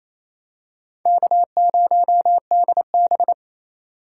Morse code, heard as a keyed tone, about 28 wpm, 700 Hz. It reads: K0DB
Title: USA Call Signs - 4 Characters - 3x Character Spacing - 28wpm